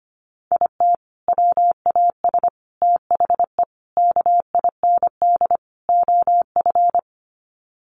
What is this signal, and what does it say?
Morse code, heard as a keyed tone, about 25 wpm, 700 Hz. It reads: IT WAS THE KIND OF
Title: Sets of 5 Words Encore2 25wpm